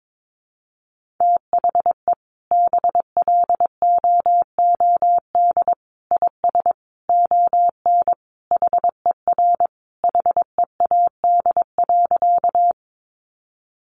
Morse code, heard as a keyed tone, about 22 wpm, 700 Hz. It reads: THE BLOOD IS ON HER HEAD.
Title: Sentences from Top 1100 Words 22wpm